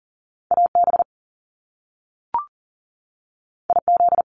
Morse code, heard as a keyed tone, about 41 wpm, 700 Hz. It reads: AB IZ
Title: Top Two Letter Combinations in Words 40wpm